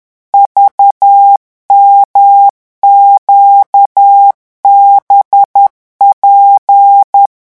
Morse code, heard as a keyed tone, about 10 wpm, 800 Hz. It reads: VMQBP